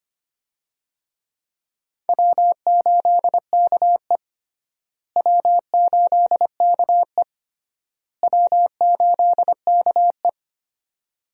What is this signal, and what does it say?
Morse code, heard as a keyed tone, about 25 wpm, 700 Hz. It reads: W8KE W8KE W8KE
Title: USA Call Signs - 4 Characters - Repeated 3x - 25wpm